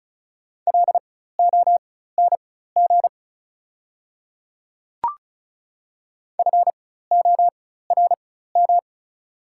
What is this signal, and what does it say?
Morse code, heard as a keyed tone, about 34 wpm, spaced as for 16 wpm, 700 Hz. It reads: LONG FORM
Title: 4x Character Spacing - Top 200 Words - 35wpm